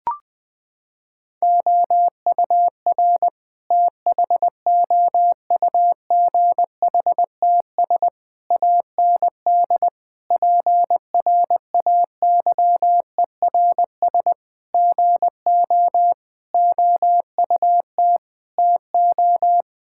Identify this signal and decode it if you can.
Morse code, heard as a keyed tone, about 21 wpm, 700 Hz. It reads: OUR THOUGHTS AND PRAYERS GO OUT TO
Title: Sentences with 8-Word Phrases - 20wpm